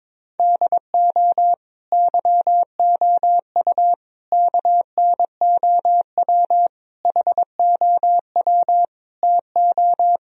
Morse code, heard as a keyed tone, about 22 wpm, 700 Hz. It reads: DO YOU KNOW HOW TO
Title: Sentences from Top 1000 Words - Review Part 3 - 22wpm